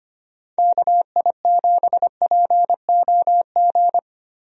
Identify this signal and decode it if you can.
Morse code, heard as a keyed tone, about 25 wpm, 700 Hz. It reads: KI7POG
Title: USA Call Signs - 6 Characters - Repeated 3x - 25wpm